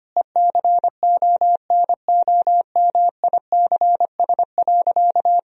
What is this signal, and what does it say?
Morse code, heard as a keyed tone, about 25 wpm, 700 Hz. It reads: ECONOMICS.
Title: Sentences from Top 1000 Words - Review Part 3 - 25wpm